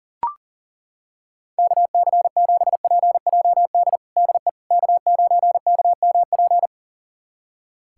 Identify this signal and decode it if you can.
Morse code, heard as a keyed tone, about 37 wpm, 700 Hz. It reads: KC7PJD DE K9KMP
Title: USA Callsign DE Callsign 40wpm